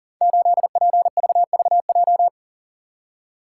Morse code, heard as a keyed tone, about 42 wpm, 700 Hz. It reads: 8PVVJ
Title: Code Groups - 5 Letter:Numbers - 40wpm